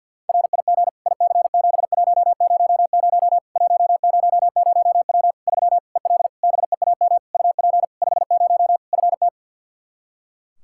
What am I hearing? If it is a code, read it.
Morse code, heard as a keyed tone, about 47 wpm, 700 Hz. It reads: RIG IC7100 100W 3 EL BEAM UP 50 FT
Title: Practice QSO - Exchange 2 - Info - 50wpm